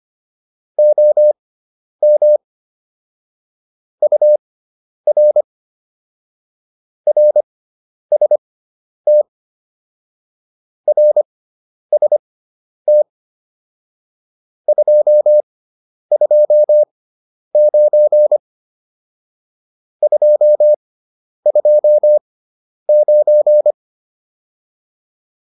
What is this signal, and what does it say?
Morse code, heard as a keyed tone, about 25 wpm, spaced as for 10 wpm, 600 Hz. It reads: OM UR RST RST 229 229